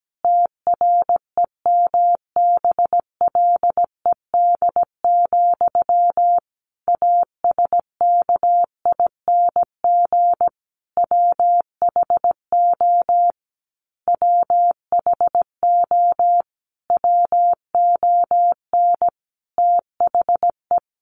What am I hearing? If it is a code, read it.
Morse code, heard as a keyed tone, about 17 wpm, 700 Hz. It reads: TREMBLED, ASKING WHO WHO WON THE
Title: A Sound Of Thunder 17wpm